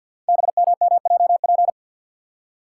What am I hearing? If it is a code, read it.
Morse code, heard as a keyed tone, about 47 wpm, 690 Hz. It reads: BMMJP
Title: Code Groups - 5 Letter:Numbers - 50wpm